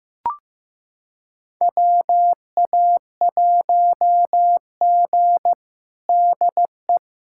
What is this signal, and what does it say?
Morse code, heard as a keyed tone, about 16 wpm, 700 Hz. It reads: WA1G DE
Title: USA Callsign DE Callsign - 4 Characters - 15wpm